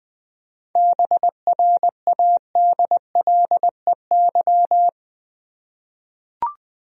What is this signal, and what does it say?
Morse code, heard as a keyed tone, about 20 wpm, 700 Hz. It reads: BRADLEY
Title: USA Common Names - 20wpm